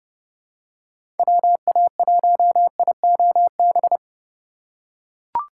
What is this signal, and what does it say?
Morse code, heard as a keyed tone, about 29 wpm, 700 Hz. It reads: WA1IOB
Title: USA Call Signs - 6 Characters - 4x Character Spacing - 30wpm